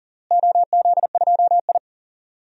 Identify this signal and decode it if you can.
Morse code, heard as a keyed tone, about 40 wpm, 700 Hz. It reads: OZ2I
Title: DXCC-100 Call Signs - 4 Characters - Repeated 2x - 40wpm